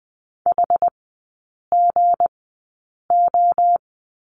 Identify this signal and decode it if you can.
Morse code, heard as a keyed tone, about 20 wpm, spaced as for 8 wpm, 690 Hz. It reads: HGO
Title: 159 Lesson 26e - Letter g - Three Characters 20wpm